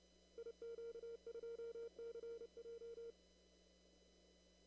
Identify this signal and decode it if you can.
Morse code, heard as a keyed tone, about 31 wpm, 460 Hz. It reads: IQ2CJ
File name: CW / kt5j wpx cw-op-k5pi